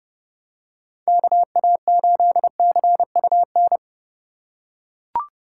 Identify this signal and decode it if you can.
Morse code, heard as a keyed tone, about 29 wpm, 700 Hz. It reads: KA8CUN